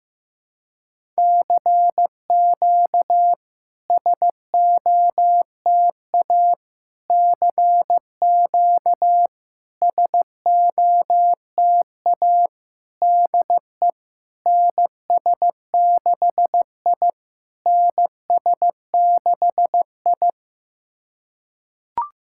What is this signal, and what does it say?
Morse code, heard as a keyed tone, about 16 wpm, 700 Hz. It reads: CQ SOTA CQ SOTA DE NS6I NS6I
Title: SOTA QSO Practice 15wpm